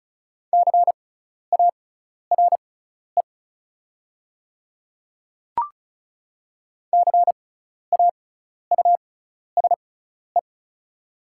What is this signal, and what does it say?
Morse code, heard as a keyed tone, about 36 wpm, spaced as for 12 wpm, 690 Hz. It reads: CARE CAUSE
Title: 6x Character Spacing - Top 300 Words - 35wpm